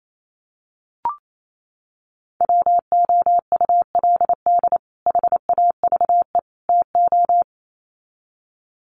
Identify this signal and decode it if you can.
Morse code, heard as a keyed tone, about 28 wpm, 700 Hz. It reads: WOULD HAVE TO
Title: Sets of 3 Words 28wpm